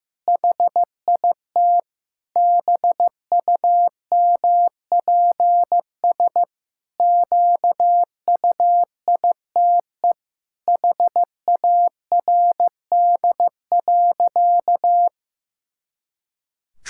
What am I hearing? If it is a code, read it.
Morse code, heard as a keyed tone, about 15 wpm, 700 Hz. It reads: HIT BUMPS QUITE HARD.
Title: Pangram Challenge v2 - 15wpm